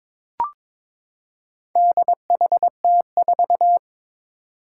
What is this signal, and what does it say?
Morse code, heard as a keyed tone, about 22 wpm, 700 Hz. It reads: DHT4
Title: Four Challenging Letter-Number Combinations 22wpm